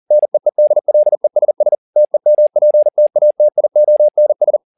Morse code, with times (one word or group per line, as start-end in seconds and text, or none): 0.1-1.7 NEEDLESS
2.0-4.6 TEMPTATIONS